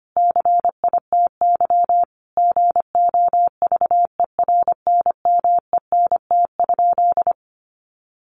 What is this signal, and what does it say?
Morse code, heard as a keyed tone, about 25 wpm, 700 Hz. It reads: CITY GOVERNMENT?
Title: Sentences from Top 1400 Words 25wpm